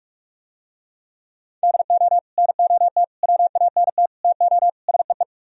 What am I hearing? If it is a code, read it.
Morse code, heard as a keyed tone, about 46 wpm, 700 Hz. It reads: DO NOT WANT TO SEE